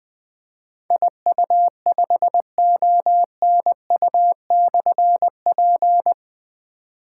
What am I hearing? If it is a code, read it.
Morse code, heard as a keyed tone, about 20 wpm, 700 Hz. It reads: IU5ONU/P
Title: Difficult Call Signs 20wpm